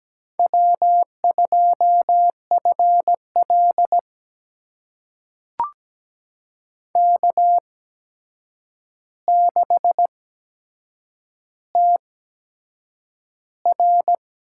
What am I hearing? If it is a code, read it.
Morse code, heard as a keyed tone, about 17 wpm, spaced as for 7 wpm, 700 Hz. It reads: W2FL K6TR
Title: USA Call Signs - 4 Characters - 8x Character Spacing - 17wpm